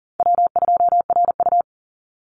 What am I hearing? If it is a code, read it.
Morse code, heard as a keyed tone, about 39 wpm, 700 Hz. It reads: W2RU
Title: Easy Call Signs Repeated 2x - 40wpm